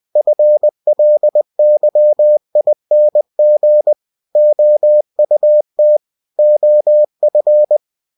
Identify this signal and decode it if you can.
Morse code, heard as a keyed tone, about 20 wpm, 600 Hz. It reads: FLYING OUT OF
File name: cw-books / winnie-the-pooh / winnie-20wpm-page-0102